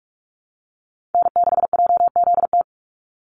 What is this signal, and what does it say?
Morse code, heard as a keyed tone, about 44 wpm, 700 Hz. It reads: N6JZT
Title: NA SOTA Activator Callsigns 45wpm